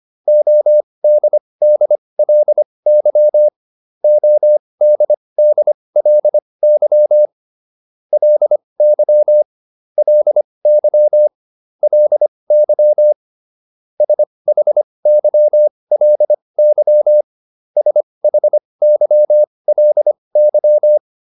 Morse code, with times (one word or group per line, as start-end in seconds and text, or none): 0.3-3.5 ODDLY
4.0-7.3 ODDLY
8.1-9.4 LY
10.0-11.3 LY
11.8-13.1 LY
14.0-17.2 SHYLY
17.8-21.0 SHYLY